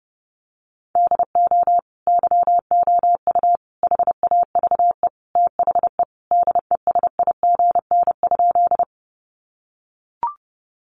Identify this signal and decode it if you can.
Morse code, heard as a keyed tone, about 30 wpm, 700 Hz. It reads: DO YOU HAVE THE DESIGN?
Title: Sentences from Top 800 Words 30wpm